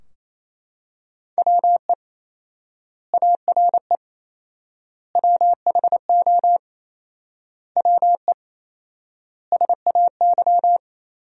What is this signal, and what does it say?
Morse code, heard as a keyed tone, about 28 wpm, 700 Hz. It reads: WE ARE WHO WE SAY
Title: 4x Word-Spacing - Sentences from Top 200 Words - 28wpm